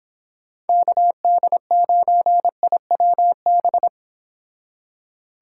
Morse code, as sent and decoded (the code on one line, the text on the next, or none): -.- -.. ----. .. .-- -...
KD9IWB